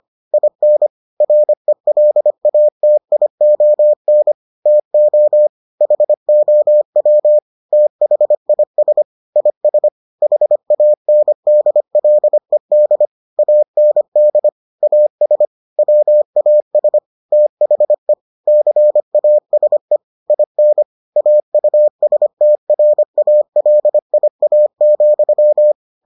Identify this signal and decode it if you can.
Morse code, heard as a keyed tone, about 25 wpm, 600 Hz. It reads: IN RELATION TO HOW THIS IS HANDLED AND AS WAS THE CASE IN AUSTRALIA,